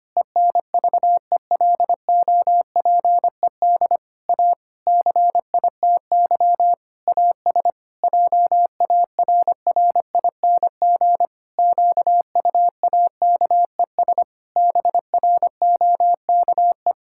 Morse code, heard as a keyed tone, about 25 wpm, 700 Hz. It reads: ENVELOPED A CITY AS JARRING QUAKES BROKE
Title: Pangram Challenge v2 - 25wpm